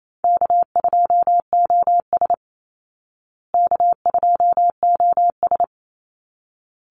Morse code, heard as a keyed tone, about 28 wpm, 700 Hz. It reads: K2OS K2OS
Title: Call Signs Repeated 3x - 28wpm